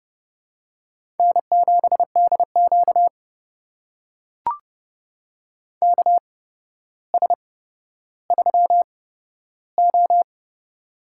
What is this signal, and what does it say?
Morse code, heard as a keyed tone, about 30 wpm, spaced as for 13 wpm, 700 Hz. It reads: N7DQ KS3O
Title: USA Call Signs - 4 Characters - 8x Character Spacing - 30wpm